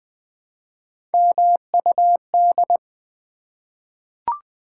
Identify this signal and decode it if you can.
Morse code, heard as a keyed tone, about 20 wpm, 700 Hz. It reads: MUD
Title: Three Letter Words - 20wpm